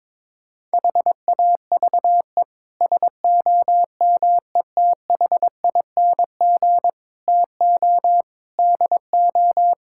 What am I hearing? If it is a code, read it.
Morse code, heard as a keyed tone, about 22 wpm, 700 Hz. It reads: HAVE SOMETHING TO DO
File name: practice / 22wpm / Sets of 8 Words - 22wpm